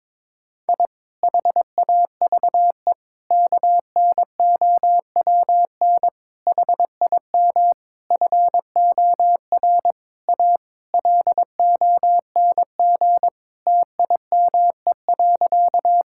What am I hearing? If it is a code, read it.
Morse code, heard as a keyed tone, about 22 wpm, 700 Hz. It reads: I HAVE KNOWN HIM FOR A LONG TIME.
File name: practice / 22wpm / Sentences with 4-Word Phrases - 22wpm